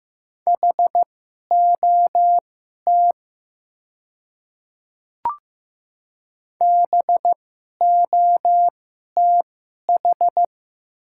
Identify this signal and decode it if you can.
Morse code, heard as a keyed tone, about 15 wpm, spaced as for 11 wpm, 700 Hz. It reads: HOT BOTH
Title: 2x Character Spacing - Top 300 Words - 15wpm